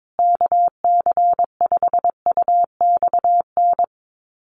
Morse code, heard as a keyed tone, about 22 wpm, 700 Hz. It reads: KC5UXN
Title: USA Call Signs Repeated 2x 22wpm